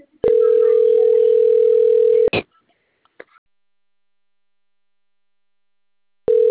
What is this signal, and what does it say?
Ringback tone (North America), 440 + 480 Hz.